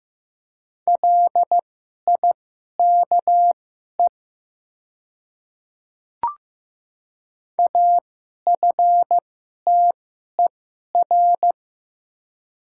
Morse code, heard as a keyed tone, about 15 wpm, spaced as for 11 wpm, 700 Hz. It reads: LIKE AFTER
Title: 2x Character Spacing - Top 300 Words - 15wpm